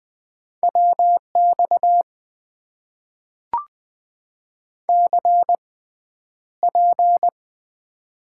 Morse code, heard as a keyed tone, about 20 wpm, spaced as for 10 wpm, 700 Hz. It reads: WX CP
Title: Top 90 QSO Elements - 6x Character Spacing - 20wpm